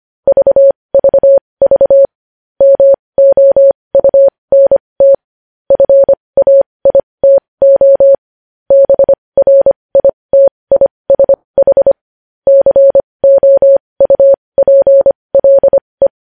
Morse code, heard as a keyed tone, about 25 wpm, spaced as for 20 wpm, 550 Hz. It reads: VVV MOUNT FAITO BRITISH COUPLE
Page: news-headlines-in-morse-code-at-20-wpm